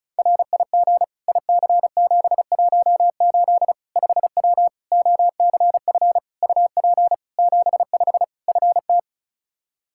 Morse code, heard as a keyed tone, about 36 wpm, 700 Hz. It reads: RIG IC718 5W OCF UP 75 FT
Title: K1USN SST QSO Practice 35wpm